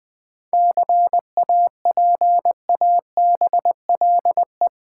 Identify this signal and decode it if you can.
Morse code, heard as a keyed tone, about 20 wpm, 690 Hz. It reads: CAPABLE
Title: 7 Letter Words - First 3 Letters Given - 20wpm